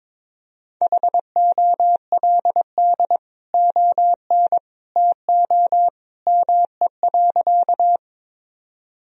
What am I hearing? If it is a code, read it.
Morse code, heard as a keyed tone, about 22 wpm, 700 Hz. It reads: HOLD ON TO ME.
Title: Sentences from Top 1000 Words - Review Part 2 - 22wpm